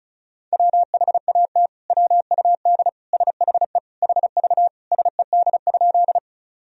Morse code, heard as a keyed tone, about 36 wpm, 700 Hz. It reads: WHAT WUD SHE HV SED?
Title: Sentences from Top 100 Words and Abbreviations 35wpm